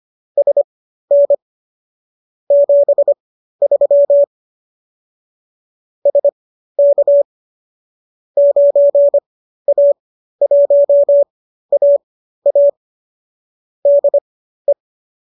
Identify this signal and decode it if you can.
Morse code, heard as a keyed tone, about 25 wpm, spaced as for 13 wpm, 570 Hz. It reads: SN 73 SK 9A1AA DE